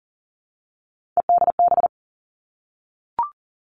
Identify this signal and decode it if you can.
Morse code, heard as a keyed tone, about 41 wpm, 690 Hz. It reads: EDB
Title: Three Challenging Letter-Number Combinations 40wpm